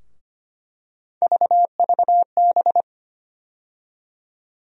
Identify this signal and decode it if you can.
Morse code, heard as a keyed tone, about 25 wpm, 700 Hz. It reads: VVB
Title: USA Call Sign Postfixes - 3 Letters 25wpm